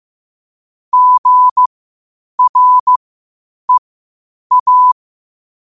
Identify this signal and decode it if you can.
Morse code, heard as a keyed tone, about 15 wpm, spaced as for 8 wpm, 1000 Hz. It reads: GREA
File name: qotdfiles / 8wpm